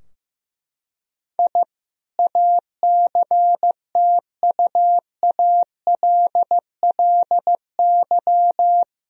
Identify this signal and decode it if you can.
Morse code, heard as a keyed tone, about 15 wpm, 700 Hz. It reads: I ACTUALLY
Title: Sentences from Top 1000 Words - Review Part 4 - 15wpm